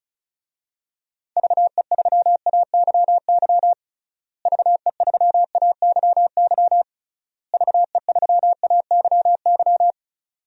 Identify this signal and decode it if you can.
Morse code, heard as a keyed tone, about 36 wpm, 700 Hz. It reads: VE3AYY VE3AYY VE3AYY
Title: DXCC-100 Call Signs - 6 Characters - Repeated 3x - 35wpm